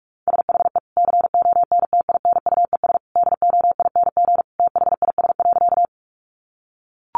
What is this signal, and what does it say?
Morse code, heard as a keyed tone, about 44 wpm, 700 Hz. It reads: SHE CONTINUES DOING THIS.
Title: Easy Phrases 45wpm